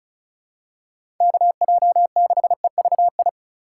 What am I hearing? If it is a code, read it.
Morse code, heard as a keyed tone, about 36 wpm, 700 Hz. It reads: KJ6EVI
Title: USA Call Signs - 6 Characters - 3x Character Spacing - 35wpm